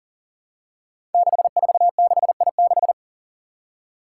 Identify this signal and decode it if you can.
Morse code, heard as a keyed tone, about 43 wpm, 700 Hz. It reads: 646I6
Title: Five Challenging Letter-Number Combinations 40wpm